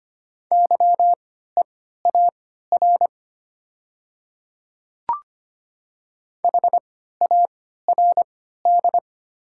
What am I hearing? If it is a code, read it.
Morse code, heard as a keyed tone, about 25 wpm, spaced as for 14 wpm, 700 Hz. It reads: YEAR HARD
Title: 3x Character Spacing - Top 300 Words - 25wpm